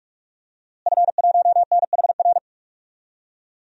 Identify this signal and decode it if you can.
Morse code, heard as a keyed tone, about 46 wpm, 700 Hz. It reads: F1NHR
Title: DXCC-100 Call Signs - 5 Characters - Repeated 2x - 45wpm